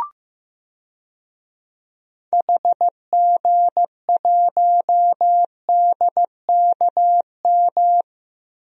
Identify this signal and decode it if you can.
Morse code, heard as a keyed tone, about 15 wpm, 700 Hz. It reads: HG1DKM
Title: DXCC-100 Call Signs - 6 Characters - Repeated 3x - 15wpm